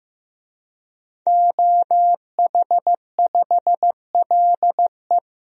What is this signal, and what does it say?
Morse code, heard as a keyed tone, about 16 wpm, 700 Hz. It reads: OH5LE
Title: DXCC-100 Call Signs - 5 Characters - Repeated 2x - 15wpm